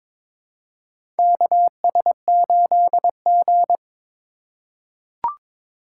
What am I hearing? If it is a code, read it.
Morse code, heard as a keyed tone, about 21 wpm, 700 Hz. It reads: KS8G
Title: USA Call Signs - 4x Character Spacing - 22wpm